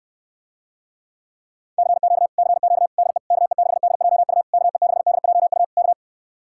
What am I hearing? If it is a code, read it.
Morse code, heard as a keyed tone, about 69 wpm, 700 Hz. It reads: CQ CQ DE KE6DJU KE6DJU K